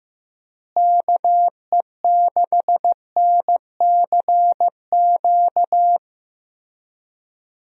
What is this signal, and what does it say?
Morse code, heard as a keyed tone, about 15 wpm, 700 Hz. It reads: KE6NCQ
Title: USA Call Signs - 6 Characters - Repeated 2x - 15wpm